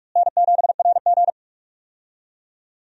Morse code, heard as a keyed tone, about 43 wpm, 700 Hz. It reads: N7RG
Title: USA Call Signs - 4 Characters - Repeated 2x - 45wpm